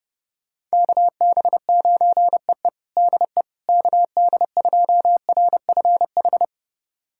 Morse code, heard as a keyed tone, about 31 wpm, 700 Hz. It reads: KB9EE DE KD2RFH
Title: USA Callsign DE Callsign 30wpm